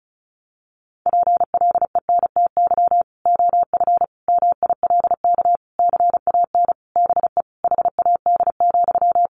Morse code, heard as a keyed tone, about 34 wpm, 700 Hz. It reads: PLENTY OF MILK CAN BE HAD,
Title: Sentences from Top 1500 Words 35wpm